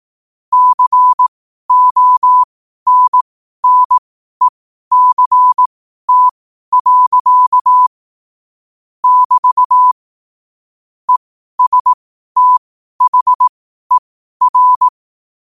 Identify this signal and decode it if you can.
Morse code, heard as a keyed tone, about 18 wpm, spaced as for 12 wpm, 1000 Hz. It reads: CONNECT. <BT> ESTHER
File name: qotdfiles / 12wpm